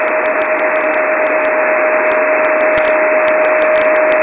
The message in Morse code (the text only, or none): none